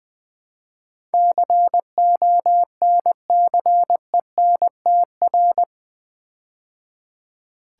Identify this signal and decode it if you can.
Morse code, heard as a keyed tone, about 20 wpm, 700 Hz. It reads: CONCENTR
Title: Build A Word - 4 Letters At A Time - 11-Letter Words - 20wpm